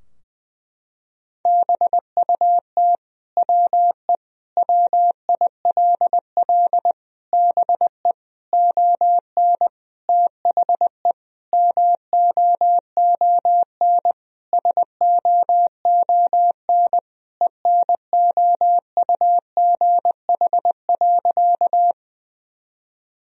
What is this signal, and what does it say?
Morse code, heard as a keyed tone, about 20 wpm, 700 Hz. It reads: BUT WE WILL BE ON THE MOON SOON ENOUGH.
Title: Sentences from Top 1000 Words - Review Part 3 - 20wpm